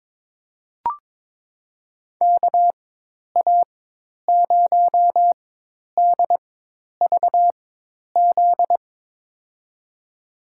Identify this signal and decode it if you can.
Morse code, heard as a keyed tone, about 22 wpm, spaced as for 10 wpm, 700 Hz. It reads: KA0DVZ
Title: USA Call Signs - 6 Characters - 4x Character Spacing - 22wpm